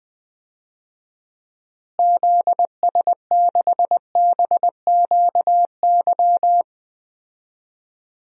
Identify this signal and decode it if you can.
Morse code, heard as a keyed tone, about 20 wpm, 700 Hz. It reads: ZS6BQY
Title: DXCC-100 Call Signs - 6 Characters - Repeated 3x - 20wpm